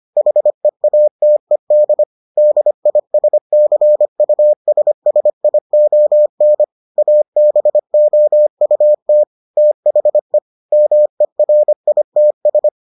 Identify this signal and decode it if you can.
Morse code, heard as a keyed tone, about 25 wpm, 600 Hz. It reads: HEATED DISCUSSION ABOUT THE MERITS